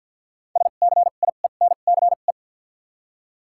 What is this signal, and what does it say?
Morse code, heard as a keyed tone, about 54 wpm, spaced as for 37 wpm, 700 Hz. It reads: SCIENCE
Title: 2x Character Spacing - Top 500 Words - 50wpm